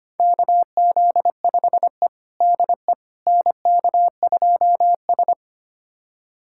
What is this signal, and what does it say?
Morse code, heard as a keyed tone, about 25 wpm, 700 Hz. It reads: KZ5E DE NK2S
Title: USA Callsign DE Callsign - 4 Characters - 25wpm